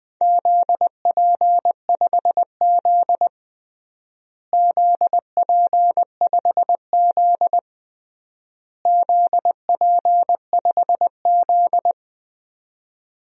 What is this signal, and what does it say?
Morse code, heard as a keyed tone, about 20 wpm, 700 Hz. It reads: ZP5Z ZP5Z ZP5Z
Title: DXCC-100 Call Signs - 4 Characters - Repeated 3x - 20wpm